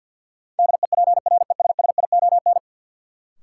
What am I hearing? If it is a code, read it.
Morse code, heard as a keyed tone, about 51 wpm, 700 Hz. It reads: DEPRESSION
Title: Sentences from Top 1300 Words - New Word Drill - 50wpm